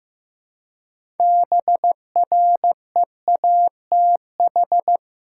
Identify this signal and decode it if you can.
Morse code, heard as a keyed tone, about 15 wpm, 700 Hz. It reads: BREATH
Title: Six Letter Words - 15wpm